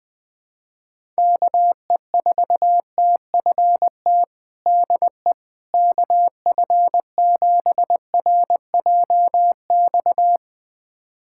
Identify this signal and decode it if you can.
Morse code, heard as a keyed tone, about 21 wpm, 700 Hz. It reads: KE4TFT DE KF7RJX
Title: USA Callsign DE Callsign - 6 Characters - 20wpm